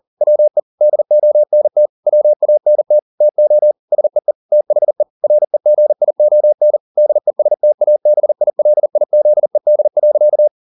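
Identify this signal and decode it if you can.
Morse code, heard as a keyed tone, about 40 wpm, 600 Hz. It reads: WE DONT WANT TO SEE THE REGION DESTABILIZED.
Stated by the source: Morse code at 40 wpm, practice text, CW med Gnister.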